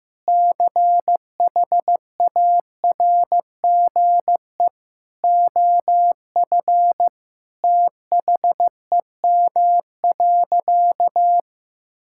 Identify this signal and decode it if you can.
Morse code, heard as a keyed tone, about 16 wpm, 700 Hz. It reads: CHARGE OF THEM.
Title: Sentences from Top 1000 Words - Review Part 6 - 15wpm